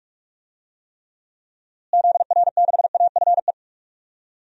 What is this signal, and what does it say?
Morse code, heard as a keyed tone, about 44 wpm, 700 Hz. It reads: ZR6AFE
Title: DXCC-100 Call Signs - 6 Characters - Repeated 2x - 45wpm